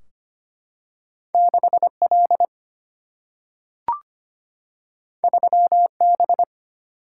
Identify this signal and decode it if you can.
Morse code, heard as a keyed tone, about 25 wpm, 700 Hz. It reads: 6 L 3 B
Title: Single-Letter Single-Number 25wpm